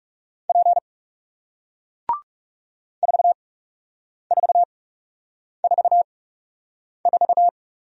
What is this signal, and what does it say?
Morse code, heard as a keyed tone, about 38 wpm, spaced as for 8 wpm, 700 Hz. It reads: P 4444